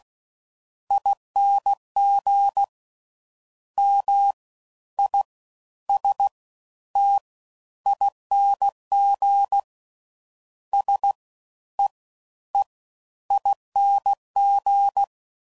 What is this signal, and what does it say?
Morse code, heard as a keyed tone, about 16 wpm, 800 Hz. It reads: ING M I S T ING S E E ING